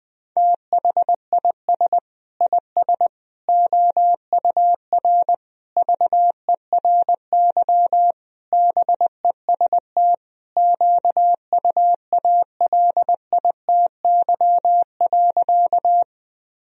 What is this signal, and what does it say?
Morse code, heard as a keyed tone, about 20 wpm, 700 Hz. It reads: THIS IS OUR VERY BEST QUALITY.
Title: Sentences from Top 1000 Words - Review Part 4 - 20wpm